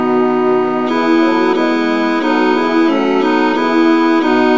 There is no code.